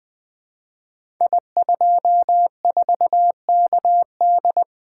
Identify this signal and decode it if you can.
Morse code, heard as a keyed tone, about 20 wpm, 700 Hz. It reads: I24KD
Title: Code Groups - 5 Letter:Numbers - 20wpm